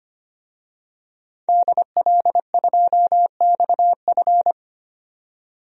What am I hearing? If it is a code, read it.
Morse code, heard as a keyed tone, about 25 wpm, 700 Hz. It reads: DL2XF
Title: DXCC-100 Call Signs - 5 Characters - 25wpm